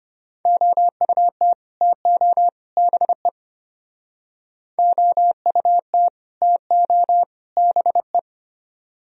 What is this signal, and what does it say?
Morse code, heard as a keyed tone, about 30 wpm, 700 Hz. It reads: OUT TO BE OUT TO BE
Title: Speed-Racing - Sets of 3 Words 25wpm